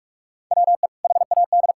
Morse code, heard as a keyed tone, about 46 wpm, 700 Hz. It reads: WE HAD
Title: Sentences from Top 1600 Words 45wpm